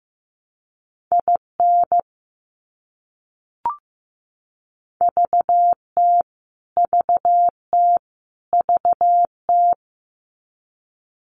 Morse code, heard as a keyed tone, about 15 wpm, 700 Hz. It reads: IN VT VT VT
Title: US States Repeated 3x 15wpm